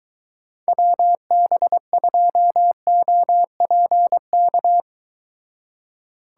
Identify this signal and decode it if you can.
Morse code, heard as a keyed tone, about 22 wpm, 700 Hz. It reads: WB2OPK